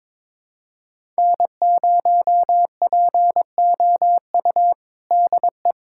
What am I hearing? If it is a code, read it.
Morse code, heard as a keyed tone, about 22 wpm, 700 Hz. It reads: N0POU DE
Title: USA Callsign DE Callsign - 5 Characters - 22wpm